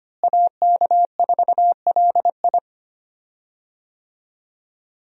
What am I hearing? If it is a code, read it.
Morse code, heard as a keyed tone, about 25 wpm, 700 Hz. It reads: AK4LI
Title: USA Call Signs - 5 Characters - Repeated 3x - 25wpm